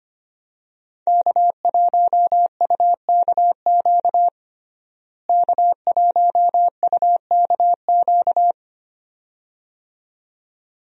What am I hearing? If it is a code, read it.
Morse code, heard as a keyed tone, about 25 wpm, 700 Hz. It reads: K1UKQ K1UKQ
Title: USA Call Signs - 5 Characters - Repeated 2x - 25wpm